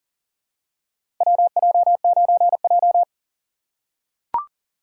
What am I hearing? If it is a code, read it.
Morse code, heard as a keyed tone, about 40 wpm, 700 Hz. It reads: WJ9J
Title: USA Call Signs - 4 Characters - 4x Character Spacing - 40wpm